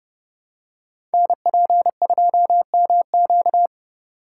Morse code, heard as a keyed tone, about 29 wpm, 700 Hz. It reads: NP2MQ
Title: Code Groups - 5 Letter:Numbers - 30wpm